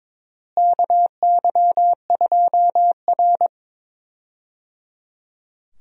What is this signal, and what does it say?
Morse code, heard as a keyed tone, about 22 wpm, 700 Hz. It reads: KY2R
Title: USA Call Signs - 4 Characters - Repeated 3x - 22wpm